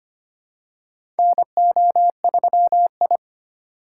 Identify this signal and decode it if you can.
Morse code, heard as a keyed tone, about 25 wpm, 700 Hz. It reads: NO3I
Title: USA Call Signs - 2x Character Spacing - 25wpm